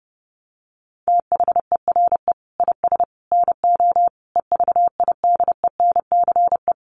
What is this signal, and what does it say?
Morse code, heard as a keyed tone, about 30 wpm, 700 Hz. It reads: THERE IS NO EVIDENCE
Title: Sentences from Top 1300 Words 30wpm